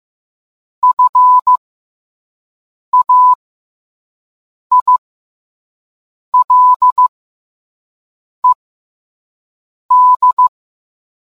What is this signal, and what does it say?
Morse code, heard as a keyed tone, about 15 wpm, spaced as for 5 wpm, 1000 Hz. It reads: FAILED